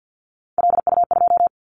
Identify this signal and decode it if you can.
Morse code, heard as a keyed tone, about 50 wpm, 700 Hz. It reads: LU2
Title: Callsign Prefixes for Top 50 DXCC - 50wpm